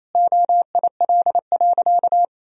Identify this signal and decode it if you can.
Morse code, heard as a keyed tone, about 28 wpm, 700 Hz. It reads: OIL.
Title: Sentences from Top 1000 Words - Review Part 4 - 28wpm